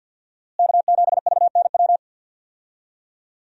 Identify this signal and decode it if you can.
Morse code, heard as a keyed tone, about 46 wpm, 700 Hz. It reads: K7VNW
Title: USA Call Signs - 5 Characters - Repeated 2x - 50wpm